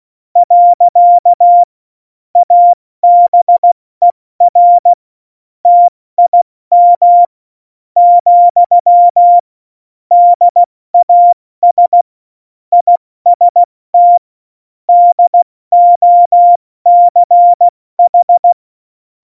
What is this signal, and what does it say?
Morse code, heard as a keyed tone, about 16 wpm, spaced as for 14 wpm, 700 Hz. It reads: . ABER TIM, DAS IST DOCH
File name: Morsekurs / Witze / Witz 21. August 2018, 70 F80